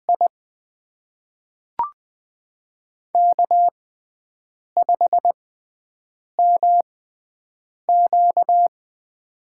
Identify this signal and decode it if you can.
Morse code, heard as a keyed tone, about 20 wpm, spaced as for 7 wpm, 700 Hz. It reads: I K5MQ
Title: USA Call Signs - 4 Characters - 6x Character Spacing - 20wpm